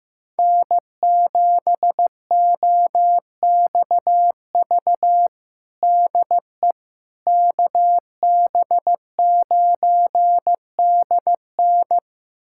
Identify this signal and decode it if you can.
Morse code, heard as a keyed tone, about 16 wpm, 700 Hz. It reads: N7OXV DE KB9DN
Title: USA Callsign DE Callsign - 5 Characters - 15wpm